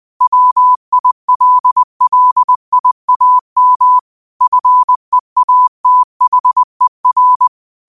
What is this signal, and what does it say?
Morse code, heard as a keyed tone, about 20 wpm, 1000 Hz. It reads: WILLIAM FEATHER
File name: qotdfiles / 20wpm